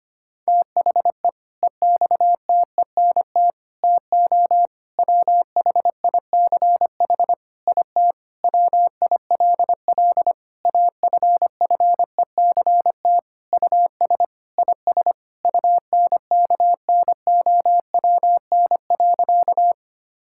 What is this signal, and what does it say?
Morse code, heard as a keyed tone, about 25 wpm, 700 Hz. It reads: THE EXTENT TO WHICH IT WILL AFFECT US IS UNKNOWN.
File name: practice / 25wpm / Sentences with 4-Word Phrases - 25wpm